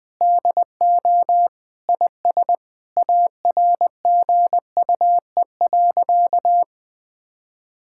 Morse code, heard as a keyed tone, about 19 wpm, 700 Hz. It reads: DO IS ARGUE.
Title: Sentences with 7-Word Phrases - Spoken Beginning - 20wpm